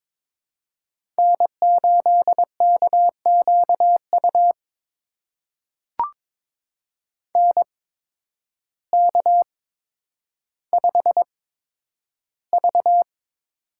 Morse code, heard as a keyed tone, about 22 wpm, spaced as for 10 wpm, 700 Hz. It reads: N8KQU NK5V
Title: USA Call Signs - 8x Character Spacing - 22wpm